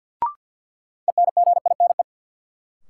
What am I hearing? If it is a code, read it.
Morse code, heard as a keyed tone, about 46 wpm, 700 Hz. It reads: ENGINE
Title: Sentences from Top 1500 Words - New Word Drill Alt - 50wpm